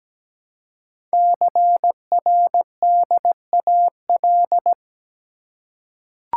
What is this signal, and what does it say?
Morse code, heard as a keyed tone, about 17 wpm, 700 Hz. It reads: CRDAL
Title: Code Groups - 5 Letters - 17wpm